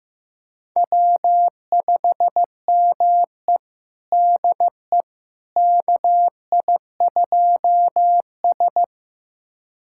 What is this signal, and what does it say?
Morse code, heard as a keyed tone, about 16 wpm, 700 Hz. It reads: W5ME DE KI2S
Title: USA Callsign DE Callsign - 4 Characters - 15wpm